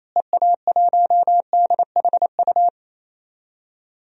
Morse code, heard as a keyed tone, about 27 wpm, 700 Hz. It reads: EA1DHU
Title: DXCC-100 Call Signs - 6 Characters - Repeated 3x - 28wpm